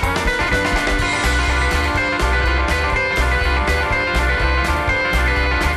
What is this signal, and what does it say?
Emergency Alert System (SAME) tones (header bursts), heard as harsh buzzing bursts. It is heard from 1.0 to 5.5 s.